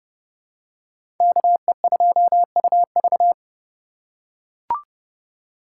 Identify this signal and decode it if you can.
Morse code, heard as a keyed tone, about 31 wpm, 700 Hz. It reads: KE2UV
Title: USA Call Signs - 5 Characters - 3x Character Spacing - 30wpm